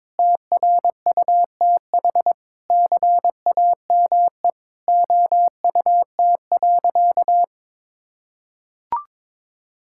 Morse code, heard as a keyed tone, about 22 wpm, 700 Hz. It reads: TRUTH CAME OUT.
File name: Sentences with 9-Word Phrases - 22wpm